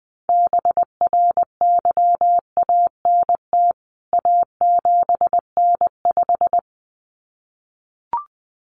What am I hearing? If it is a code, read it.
Morse code, heard as a keyed tone, about 20 wpm, 700 Hz. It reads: BRYANT A7N5
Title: CWT Exchange 20wpm